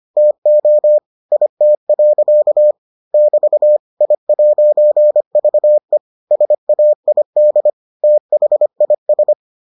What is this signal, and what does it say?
Morse code, heard as a keyed tone, about 25 wpm, 600 Hz. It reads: TO IT. <BT> I'VE SAID THIS